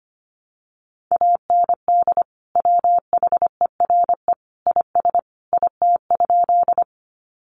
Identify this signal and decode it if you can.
Morse code, heard as a keyed tone, about 25 wpm, 700 Hz. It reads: AND WHERE IS IT?
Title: Sentences from Top 200 Words 25wpm